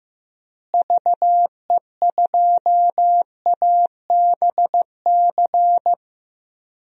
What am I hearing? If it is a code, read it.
Morse code, heard as a keyed tone, about 15 wpm, 700 Hz. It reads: VE2ABC
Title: DXCC-100 Call Signs - 6 Characters - Repeated 3x - 15wpm